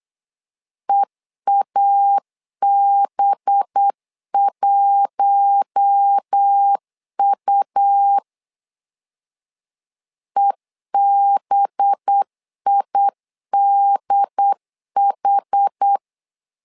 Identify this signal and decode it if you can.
Morse code, heard as a keyed tone, about 8 wpm, 800 Hz. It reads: EAB1U EBIDH
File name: LEZIONI / 5 LBVC / 5 LB 8wpm 2816